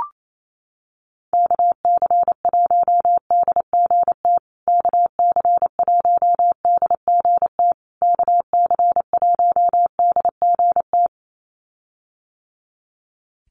Morse code, heard as a keyed tone, about 28 wpm, 700 Hz. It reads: KC1DGT KC1DGT KC1DGT
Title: USA Call Signs Repeated 3x 28wpm